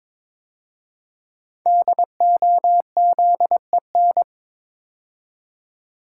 Morse code, heard as a keyed tone, about 22 wpm, 700 Hz. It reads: DOZEN
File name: 5 Letter Words - First 3 Letters Given - 22wpm